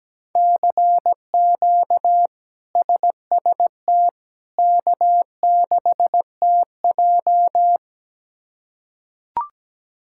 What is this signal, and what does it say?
Morse code, heard as a keyed tone, about 17 wpm, 700 Hz. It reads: CQ SST K6TJ
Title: K1USN SST QSO Practice - 17wpm